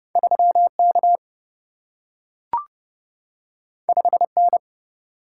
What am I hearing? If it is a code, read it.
Morse code, heard as a keyed tone, about 30 wpm, 700 Hz. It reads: 3 K 5 N
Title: Single-Letter Single-Number 30wpm